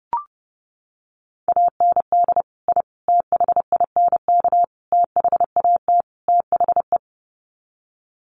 Morse code, heard as a keyed tone, about 30 wpm, 700 Hz. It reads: AND I THINK THAT THE
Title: Sets of 5 Words 30wpm